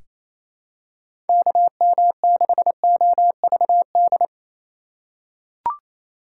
Morse code, heard as a keyed tone, about 28 wpm, 700 Hz. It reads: KM6OVD
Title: USA Call Signs - 6 Characters - 3x Character Spacing - 28wpm